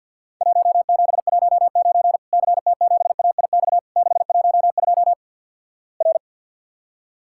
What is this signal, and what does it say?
Morse code, heard as a keyed tone, about 51 wpm, 700 Hz. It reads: 1719 CT7AIX 612 R
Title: Worked All Europe DX Contest - QTC Exchange - 50wpm